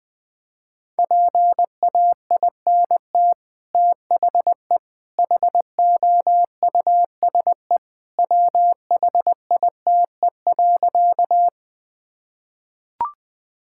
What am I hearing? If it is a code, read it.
Morse code, heard as a keyed tone, about 20 wpm, 700 Hz. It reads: PAINT THE HOUSE WHITE.
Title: Sentences from Top 1000 Words - Review Part 6 - 20wpm